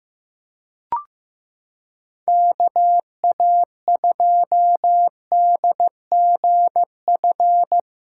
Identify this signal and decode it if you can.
Morse code, heard as a keyed tone, about 15 wpm, 700 Hz. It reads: KA2DGF
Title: USA Callsign DE Callsign - 6 Characters - 15wpm